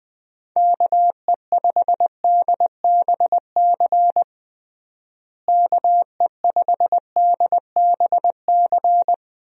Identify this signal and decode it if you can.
Morse code, heard as a keyed tone, about 20 wpm, 700 Hz. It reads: KE5DBC KE5DBC
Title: USA Call Signs - 6 Characters - Repeated 2x - 20wpm